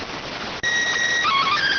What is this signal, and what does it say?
SSTV (slow-scan television), PD 290.